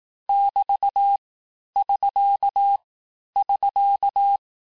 Morse code, heard as a keyed tone, about 18 wpm, 800 Hz. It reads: <BT> <SK> <SK>
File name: teste abreviations vitesse 17mpm puis 22mpm